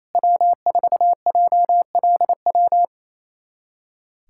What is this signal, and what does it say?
Morse code, heard as a keyed tone, about 28 wpm, 700 Hz. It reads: W4JLW
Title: USA Callsign DE Callsign 28wpm